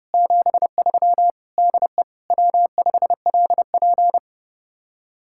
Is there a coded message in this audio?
Yes: Morse code from 0.1 to 4.2 s, heard as a keyed tone, about 29 wpm, 700 Hz. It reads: 73 DE W5LP